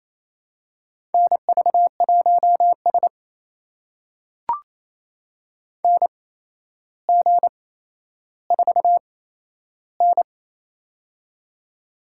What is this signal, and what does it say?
Morse code, heard as a keyed tone, about 28 wpm, spaced as for 12 wpm, 700 Hz. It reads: NV1S NG4N
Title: USA Call Signs - 4 Characters - 8x Character Spacing - 28wpm